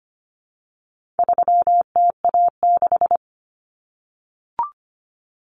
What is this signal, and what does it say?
Morse code, heard as a keyed tone, about 25 wpm, 700 Hz. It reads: 3TA6
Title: Common Cut Numbers - 4-Digit 25wpm